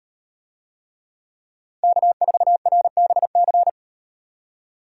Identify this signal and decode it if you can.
Morse code, heard as a keyed tone, about 38 wpm, 700 Hz. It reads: K4RBC